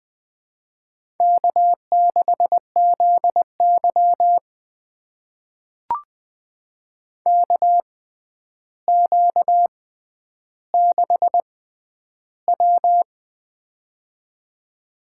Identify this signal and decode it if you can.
Morse code, heard as a keyed tone, about 21 wpm, spaced as for 10 wpm, 700 Hz. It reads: K6ZY KQ6W
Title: USA Call Signs - 4 Characters - 6x Character Spacing - 20wpm